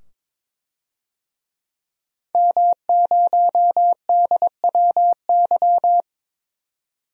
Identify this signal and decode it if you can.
Morse code, heard as a keyed tone, about 22 wpm, 700 Hz. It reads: M0DWY
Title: DXCC-100 Call Signs - 5 Characters - 22wpm